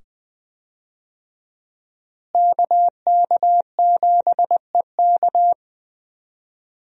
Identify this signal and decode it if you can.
Morse code, heard as a keyed tone, about 20 wpm, 700 Hz. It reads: KK7EK
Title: USA Call Signs - 5 Characters - Repeated 2x - 20wpm